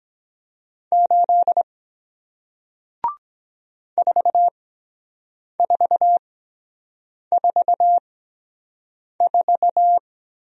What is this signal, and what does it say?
Morse code, heard as a keyed tone, about 23 wpm, spaced as for 7 wpm, 700 Hz. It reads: 8 4444